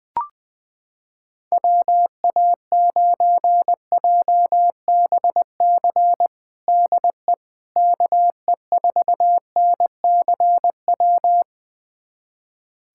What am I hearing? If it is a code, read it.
Morse code, heard as a keyed tone, about 20 wpm, 700 Hz. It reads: WA9JBC DE KE4NCW
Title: USA Callsign DE Callsign - 6 Characters - 20wpm